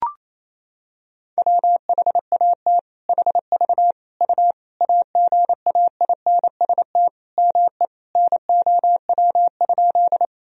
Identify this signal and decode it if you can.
Morse code, heard as a keyed tone, about 28 wpm, 700 Hz. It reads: WHAT HV U AGAINST ME NOW?
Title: Sentences from Top 400 Words and Abbreviations 28wpm